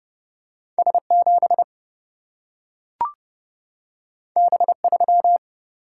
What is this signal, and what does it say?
Morse code, heard as a keyed tone, about 30 wpm, 700 Hz. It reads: S 7 B 3
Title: Single-Letter Single-Number 30wpm